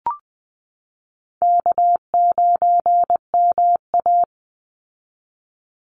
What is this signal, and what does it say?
Morse code, heard as a keyed tone, about 20 wpm, 700 Hz. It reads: K9MA
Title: Call Signs Repeated 2x - 20wpm